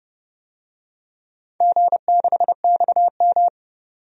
Morse code, heard as a keyed tone, about 30 wpm, 700 Hz. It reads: G6XM
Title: DXCC-100 Call Signs - 4 Characters - 30wpm